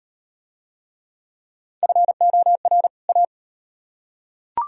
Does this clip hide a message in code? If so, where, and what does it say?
Yes: Morse code from 1.8 to 3.2 s, heard as a keyed tone, about 34 wpm, 700 Hz. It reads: FOR A